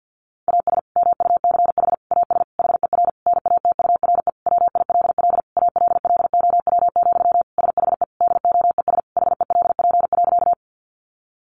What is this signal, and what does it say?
Morse code, heard as a keyed tone, about 50 wpm, 700 Hz. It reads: AS MUCH AS HER NATURE WILL ALLOW, SHE DOES HELP.
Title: Sentences from Top 700 Words 50wpm